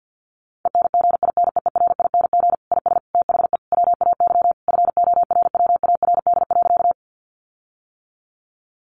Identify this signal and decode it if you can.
Morse code, heard as a keyed tone, about 50 wpm, 700 Hz. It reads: ENGINEERING IS THE WAY FORWARD.
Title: Sentences from Top 1500 Words 50wpm